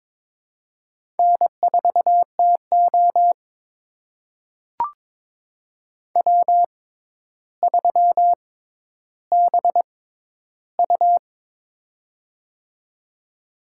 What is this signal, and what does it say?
Morse code, heard as a keyed tone, about 22 wpm, spaced as for 11 wpm, 700 Hz. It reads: N4TO W3BU
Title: USA Call Signs - 4 Characters - 6x Character Spacing - 22wpm